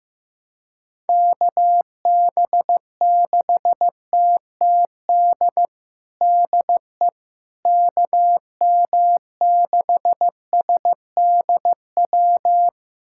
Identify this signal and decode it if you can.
Morse code, heard as a keyed tone, about 15 wpm, 700 Hz. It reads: KB6TTD DE KM6SDW
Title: USA Callsign DE Callsign - 6 Characters - 15wpm